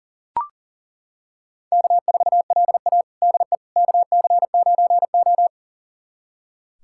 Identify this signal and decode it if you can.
Morse code, heard as a keyed tone, about 40 wpm, 700 Hz. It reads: K4LA DE KC9O
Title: USA Callsign DE Callsign - 4 Characters - 40wpm